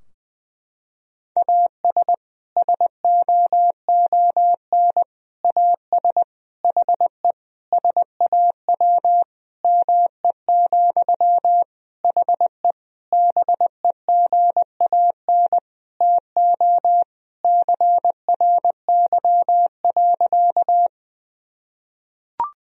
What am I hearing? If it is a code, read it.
Morse code, heard as a keyed tone, about 20 wpm, 700 Hz. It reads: AS SOON AS HE SAW ME, HE BEGAN TO CRY.
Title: Sentences from Top 1000 Words - Review Part 2 - 20wpm